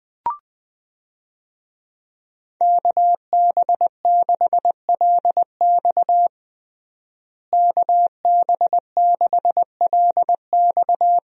Morse code, heard as a keyed tone, about 20 wpm, 700 Hz. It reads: KB6LX KB6LX
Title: USA Call Signs - 5 Characters - Repeated 2x - 20wpm